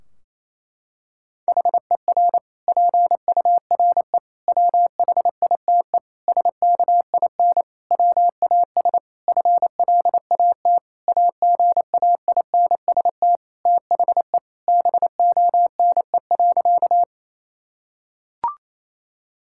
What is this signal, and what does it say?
Morse code, heard as a keyed tone, about 28 wpm, 700 Hz. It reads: HER PURE WHITE SKIN WAS FLAT AGAINST THE BONE.
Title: Sentences from Top 1600 Words 28wpm